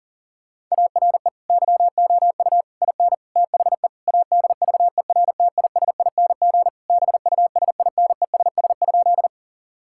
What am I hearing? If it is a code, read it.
Morse code, heard as a keyed tone, about 40 wpm, 700 Hz. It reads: ARE YOU IN THE ADVERTISING BUSINESS?
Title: Sentences from Top 1000 Words - Review Part 6 - 40wpm